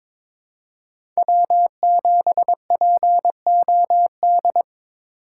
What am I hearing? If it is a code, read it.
Morse code, heard as a keyed tone, about 21 wpm, 700 Hz. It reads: W7POD
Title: USA Call Signs - 5 Characters - 2x Character Spacing - 22wpm